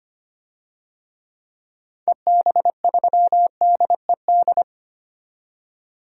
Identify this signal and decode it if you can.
Morse code, heard as a keyed tone, about 25 wpm, 700 Hz. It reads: EB3DED
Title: DXCC-100 Call Signs - 6 Characters - 25wpm